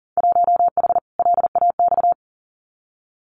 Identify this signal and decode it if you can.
Morse code, heard as a keyed tone, about 42 wpm, 700 Hz. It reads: 1H LAX
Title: Winter Field Day Exchanges 40wpm